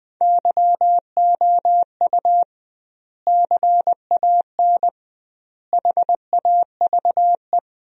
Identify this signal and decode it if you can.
Morse code, heard as a keyed tone, about 20 wpm, 700 Hz. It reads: YOU CAN HAVE
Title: 2x Word-Spacing - Sentences from Top 100 Words - 20wpm